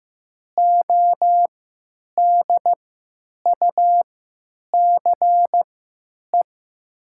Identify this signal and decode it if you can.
Morse code, heard as a keyed tone, about 15 wpm, spaced as for 9 wpm, 700 Hz. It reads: ODUCE
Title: 3x Character Spacing - Top 500 Words - 15wpm